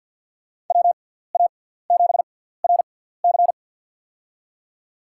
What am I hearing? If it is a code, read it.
Morse code, heard as a keyed tone, about 52 wpm, spaced as for 17 wpm, 700 Hz. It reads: WA7RC
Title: USA Call Signs - 5 Characters - 6x Character Spacing - 50wpm